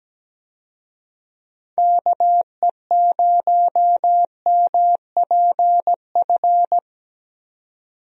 Morse code, heard as a keyed tone, about 17 wpm, 700 Hz. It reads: KE0MPF
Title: USA Call Signs - 6 Characters - 17wpm